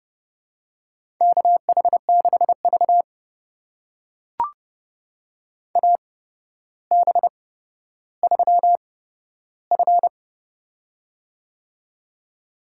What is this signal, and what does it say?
Morse code, heard as a keyed tone, about 30 wpm, spaced as for 13 wpm, 700 Hz. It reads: KH6V AB3F
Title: USA Call Signs - 4 Characters - 8x Character Spacing - 30wpm